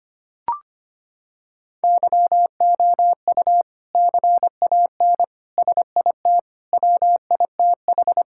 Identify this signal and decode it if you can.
Morse code, heard as a keyed tone, about 25 wpm, 700 Hz. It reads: YOU CAN SIT WITH
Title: Sentences from Top 1000 Words - Review Part 5 - 25wpm